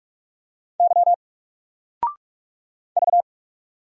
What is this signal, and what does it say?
Morse code, heard as a keyed tone, about 44 wpm, 700 Hz. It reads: Y V
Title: Single Letter-Number 45wpm